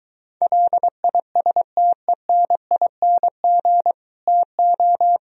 Morse code, heard as a keyed tone, about 23 wpm, 700 Hz. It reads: LISTENING TO